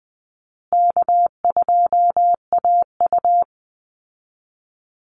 Morse code, heard as a keyed tone, about 20 wpm, 700 Hz. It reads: K2AU
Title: Easy Call Signs Repeated 3x - 20wpm